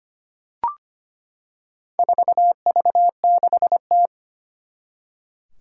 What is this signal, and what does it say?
Morse code, heard as a keyed tone, about 25 wpm, 700 Hz. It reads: 4V6T
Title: Four Challenging Letter-Number Combinations 25wpm